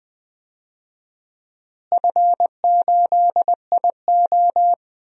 Morse code, heard as a keyed tone, about 20 wpm, 690 Hz. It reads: F8IO
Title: DXCC-100 Call Signs - 4 Characters - Repeated 3x - 20wpm